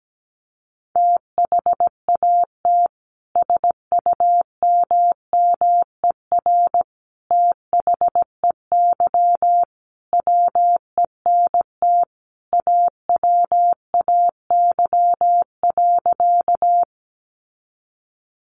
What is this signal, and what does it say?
Morse code, heard as a keyed tone, about 17 wpm, 700 Hz. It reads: THAT SUMMER THEY WENT AWAY.
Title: Sentences from Top 1100 Words 17wpm